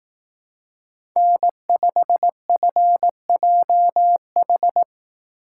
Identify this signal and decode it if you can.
Morse code, heard as a keyed tone, about 18 wpm, 700 Hz. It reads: N5FJH
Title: Speed Racing - USA 1x3 Call Signs 15wpm